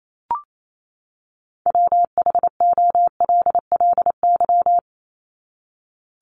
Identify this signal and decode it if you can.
Morse code, heard as a keyed tone, about 28 wpm, 700 Hz. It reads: WHOLLY
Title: LY Suffix - Top 100 Words with Base Words 28wpm